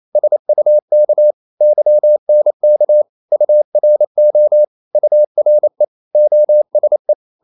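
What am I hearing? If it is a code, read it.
Morse code, heard as a keyed tone, about 28 wpm, 600 Hz. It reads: SUK YNK URO URE OSE
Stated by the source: Danish short words sent in Morse at 28 wpm, from CW med Gnister.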